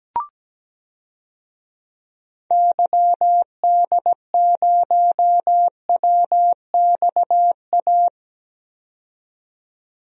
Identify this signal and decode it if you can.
Morse code, heard as a keyed tone, about 17 wpm, 700 Hz. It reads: YD0WXA
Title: DXCC-100 Call Signs - 6 Characters - 17wpm